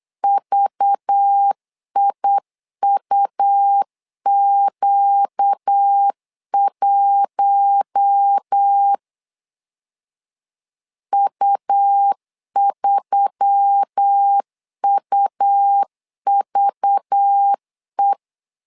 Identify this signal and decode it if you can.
Morse code, heard as a keyed tone, about 8 wpm, 800 Hz. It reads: VIUQ1 U3UVE
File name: LEZIONI / 5 LBVC / 5 LBV 8wpm 2837